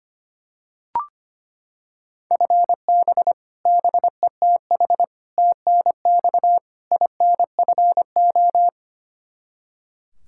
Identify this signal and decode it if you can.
Morse code, heard as a keyed tone, about 25 wpm, 700 Hz. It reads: FB BETH TNX INFO
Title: Practice QSO - Exchange 3 - More Info - 25wpm